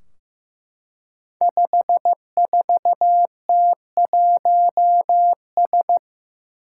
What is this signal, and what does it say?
Morse code, heard as a keyed tone, about 15 wpm, 700 Hz. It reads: 54T1S
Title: Five Challenging Letter-Number Combinations 15wpm